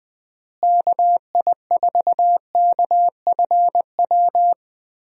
Morse code, heard as a keyed tone, about 20 wpm, 700 Hz. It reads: KI4KFW
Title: USA Call Signs - 6 Characters - Repeated 3x - 20wpm